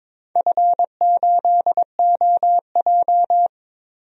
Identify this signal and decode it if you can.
Morse code, heard as a keyed tone, about 22 wpm, 700 Hz. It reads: F8OJ
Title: DXCC-100 Call Signs - 4 Characters - 22wpm